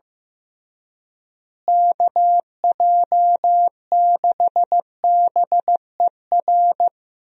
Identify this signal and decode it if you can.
Morse code, heard as a keyed tone, about 15 wpm, 700 Hz. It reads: KJ6BER
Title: USA Call Signs - 6 Characters - Repeated 3x - 15wpm